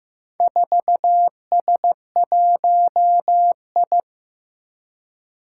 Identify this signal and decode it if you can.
Morse code, heard as a keyed tone, about 15 wpm, 700 Hz. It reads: 4S1I
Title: Four Challenging Letter-Number Combinations 15wpm